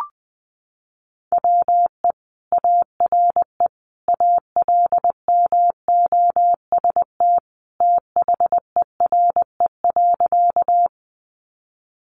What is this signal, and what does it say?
Morse code, heard as a keyed tone, about 20 wpm, 700 Hz. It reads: WE ARE ALMOST THERE.
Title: Sentences from Top 700 Words 20wpm